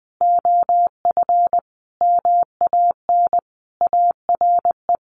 Morse code, heard as a keyed tone, about 20 wpm, 700 Hz. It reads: OF MAN ARE
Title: Sentences from Top 1300 Words 20wpm